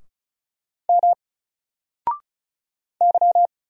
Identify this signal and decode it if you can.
Morse code, heard as a keyed tone, about 33 wpm, 700 Hz. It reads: M Y
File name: Single-Letter-Number-35wpm